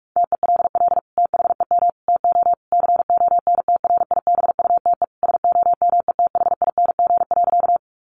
Morse code, heard as a keyed tone, about 44 wpm, 700 Hz. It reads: TELL THEM TO CONTRIBUTE SOMETHING.
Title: Sentences from Top 1400 Words 45wpm